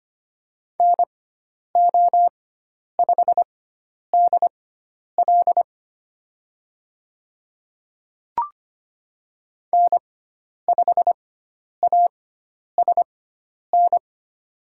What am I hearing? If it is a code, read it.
Morse code, heard as a keyed tone, about 25 wpm, spaced as for 10 wpm, 700 Hz. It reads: NO5DL N5ASN
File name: Lesson 16g - Letter c - Call Signs 25wpm